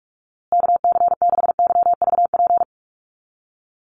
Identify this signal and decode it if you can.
Morse code, heard as a keyed tone, about 44 wpm, 700 Hz. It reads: KC6YVP
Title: USA Call Signs Repeated 3x 45wpm